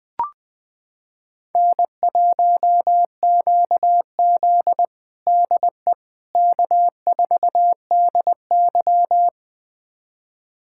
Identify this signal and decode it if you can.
Morse code, heard as a keyed tone, about 20 wpm, 700 Hz. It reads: N1QZ DE K4DY
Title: USA Callsign DE Callsign - 4 Characters - 20wpm